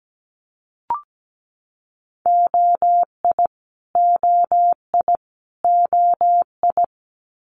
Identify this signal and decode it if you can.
Morse code, heard as a keyed tone, about 17 wpm, 700 Hz. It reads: OI OI OI
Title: Multi-letter Phonemes Repeated 3x 17wpm